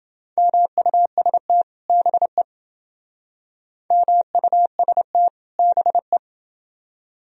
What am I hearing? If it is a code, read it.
Morse code, heard as a keyed tone, about 29 wpm, 700 Hz. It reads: MUST BE MUST BE